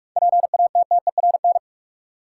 Morse code, heard as a keyed tone, about 47 wpm, 700 Hz. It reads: PATTERN
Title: Sentences from Top 1000 Words - New Word Drill - 45wpm